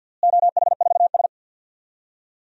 Morse code, heard as a keyed tone, about 52 wpm, 700 Hz. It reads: OH4S
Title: DXCC-100 Call Signs - 4 Characters - Repeated 3x - 50wpm